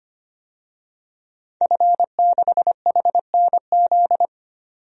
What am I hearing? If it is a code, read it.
Morse code, heard as a keyed tone, about 25 wpm, 700 Hz. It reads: F6HNZ
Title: DXCC-100 Call Signs - 5 Characters - 25wpm